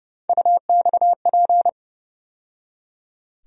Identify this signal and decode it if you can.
Morse code, heard as a keyed tone, about 31 wpm, 700 Hz. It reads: UXP
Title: USA Call Sign Postfixes - 3 Letters 30wpm